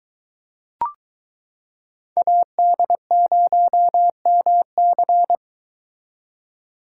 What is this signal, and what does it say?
Morse code, heard as a keyed tone, about 22 wpm, 700 Hz. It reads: AD0MC